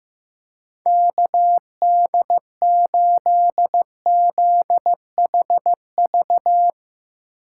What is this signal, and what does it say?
Morse code, heard as a keyed tone, about 15 wpm, 700 Hz. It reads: KD8ZHV
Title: Speed Racing - USA 2x3 Call Signs 15wpm